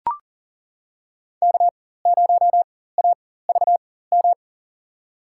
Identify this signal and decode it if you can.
Morse code, heard as a keyed tone, about 40 wpm, spaced as for 19 wpm, 700 Hz. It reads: K0AVM
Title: USA Call Signs - 5 Characters - 4x Character Spacing - 40wpm